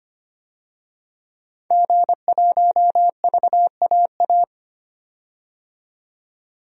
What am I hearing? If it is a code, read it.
Morse code, heard as a keyed tone, about 25 wpm, 700 Hz. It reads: G1VAA
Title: DXCC-100 Call Signs - 5 Characters - 25wpm